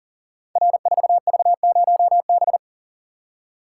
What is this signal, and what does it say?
Morse code, heard as a keyed tone, about 40 wpm, 700 Hz. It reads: R4V0B
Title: Code Groups - 5 Letter:Numbers - 40wpm